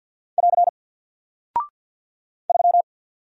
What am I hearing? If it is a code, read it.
Morse code, heard as a keyed tone, about 49 wpm, 700 Hz. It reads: <AR> 3
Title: Single Character (Letter:Number:Punctuation:Prosign) 50wpm